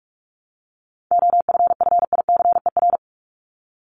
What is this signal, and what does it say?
Morse code, heard as a keyed tone, about 44 wpm, 700 Hz. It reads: OFFICER
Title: Top 1001-1250 Words 45wpm